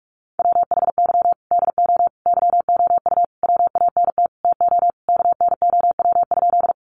Morse code, heard as a keyed tone, about 44 wpm, 700 Hz. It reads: WHY DO YOU WANT TO KNOW?
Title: Sentences from Top 200 Words 45wpm